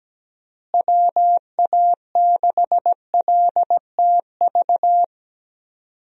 Morse code, heard as a keyed tone, about 17 wpm, 700 Hz. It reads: WA6LTV
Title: USA Call Signs - 6 Characters - Repeated 3x - 17wpm